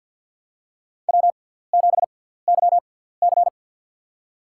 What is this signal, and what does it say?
Morse code, heard as a keyed tone, about 46 wpm, spaced as for 17 wpm, 700 Hz. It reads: W7YC
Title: USA Call Signs - 4 Characters - 6x Character Spacing - 50wpm